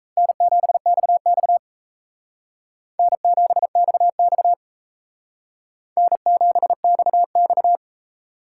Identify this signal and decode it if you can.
Morse code, heard as a keyed tone, about 34 wpm, 700 Hz. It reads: N7XX N7XX N7XX
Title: Speed Racing - USA 1x2 Call Signs 28wpm